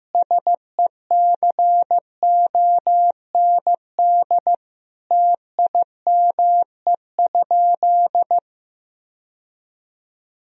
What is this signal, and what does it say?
Morse code, heard as a keyed tone, about 16 wpm, 700 Hz. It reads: SECOND TIME?
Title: Sentences from Top 1000 Words - Review Part 1 - 15wpm